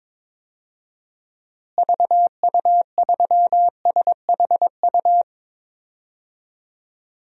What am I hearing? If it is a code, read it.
Morse code, heard as a keyed tone, about 22 wpm, 700 Hz. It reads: VU3SHU
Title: DXCC-100 Call Signs - 6 Characters - Repeated 2x - 22wpm